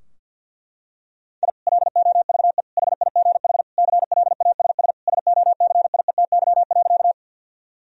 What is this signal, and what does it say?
Morse code, heard as a keyed tone, about 46 wpm, 700 Hz. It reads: I LOVE HIGH CLASS SOCIETY.
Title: Sentences from Top 1000 Words - Review Part 4 - 50wpm